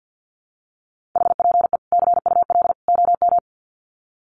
Morse code, heard as a keyed tone, about 48 wpm, 700 Hz. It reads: HPE CUL OM
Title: Common QSO Phrases 50wpm